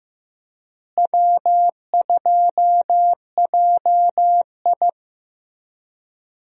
Morse code, heard as a keyed tone, about 15 wpm, 700 Hz. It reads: W2JI
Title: USA Call Signs - 4 Characters - Repeated 2x - 15wpm